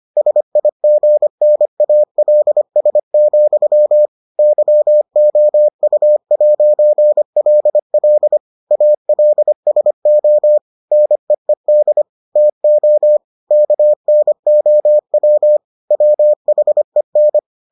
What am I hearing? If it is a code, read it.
Morse code, heard as a keyed tone, about 25 wpm, 600 Hz. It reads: SIGNALS, YOU'LL ALSO NEED TO KNOW WHEN